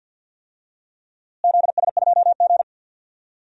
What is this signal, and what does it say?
Morse code, heard as a keyed tone, about 50 wpm, 700 Hz. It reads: ZS2G
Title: DXCC-100 Call Signs - 4 Characters - Repeated 2x - 50wpm